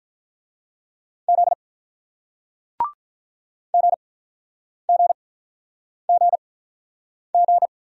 Morse code, heard as a keyed tone, about 47 wpm, spaced as for 9 wpm, 700 Hz. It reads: Z GGGG